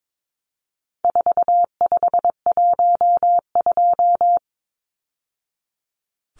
Morse code, heard as a keyed tone, about 22 wpm, 700 Hz. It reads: 4512
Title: Numbers - 4-Digit 22wpm